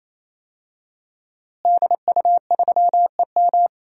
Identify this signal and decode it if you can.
Morse code, heard as a keyed tone, about 28 wpm, 700 Hz. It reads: DU3EM
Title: DXCC-100 Call Signs - 5 Characters - Repeated 3x - 28wpm